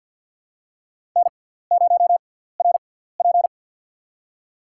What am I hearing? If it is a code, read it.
Morse code, heard as a keyed tone, about 53 wpm, spaced as for 17 wpm, 690 Hz. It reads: N0RP
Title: USA Call Signs - 4 Characters - 6x Character Spacing - 50wpm